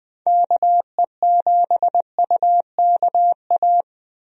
Morse code, heard as a keyed tone, about 20 wpm, 700 Hz. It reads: KE7UKA
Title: USA Call Signs - 6 Characters - Repeated 2x - 20wpm